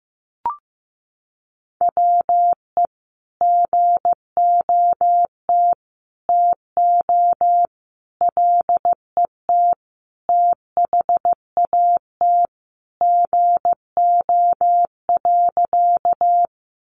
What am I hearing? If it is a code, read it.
Morse code, heard as a keyed tone, about 15 wpm, 700 Hz. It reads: WE GOT TO LET THAT GO.
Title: Sentences from Top 300 Words 15wpm